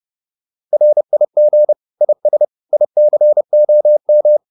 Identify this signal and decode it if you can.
Morse code, heard as a keyed tone, about 30 wpm, 600 Hz. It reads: RIG IS ICOM